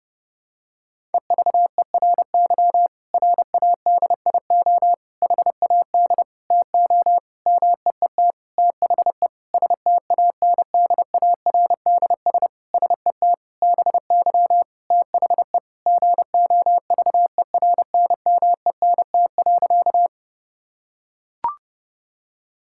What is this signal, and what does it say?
Morse code, heard as a keyed tone, about 30 wpm, 700 Hz. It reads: EVERY RADIO HAD TO MEET THE STANDARDS SET BY THE GOVERNMENT.
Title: Sentences from Top 1000 Words - Review Part 6 - 30wpm